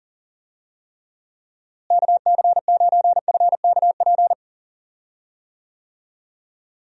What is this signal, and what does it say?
Morse code, heard as a keyed tone, about 38 wpm, 700 Hz. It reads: KC9FKP
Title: USA Call Signs - 6 Characters - Repeated 2x - 40wpm